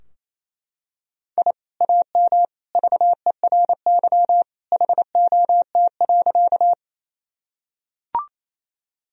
Morse code, heard as a keyed tone, about 28 wpm, 700 Hz. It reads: I AM VERY HOT.
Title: Sentences from Top 1000 Words - Review Part 3 - 28wpm